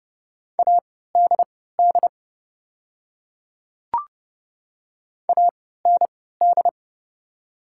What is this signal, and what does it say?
Morse code, heard as a keyed tone, about 30 wpm, spaced as for 17 wpm, 690 Hz. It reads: ADD AND
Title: 3x Character Spacing - Top 200 Words - 30wpm